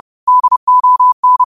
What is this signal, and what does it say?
Morse code, heard as a keyed tone, about 30 wpm, 1000 Hz. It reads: DON